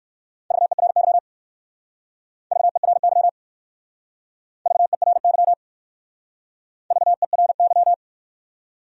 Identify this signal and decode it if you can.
Morse code, heard as a keyed tone, about 59 wpm, 700 Hz. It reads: VERY VERY VERY VERY